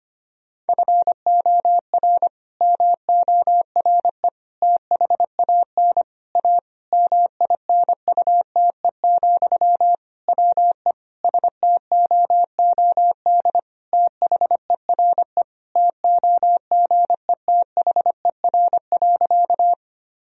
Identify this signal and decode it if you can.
Morse code, heard as a keyed tone, about 25 wpm, 700 Hz. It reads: FOR MORE THAN A MINUTE, WE STOOD THERE TOGETHER.
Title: Sentences from Top 1000 Words - Review Part 2 - 25wpm